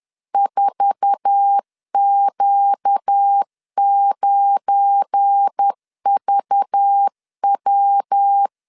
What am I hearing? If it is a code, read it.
Morse code, heard as a keyed tone, about 10 wpm, 800 Hz. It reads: 4Q9VW
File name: CW / DET / 1 9 1500C S B 10wpm 4439